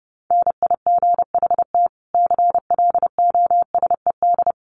Morse code, heard as a keyed tone, about 30 wpm, 700 Hz. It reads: NIGHT CLOSED
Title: Sentences from Top 1400 Words 30wpm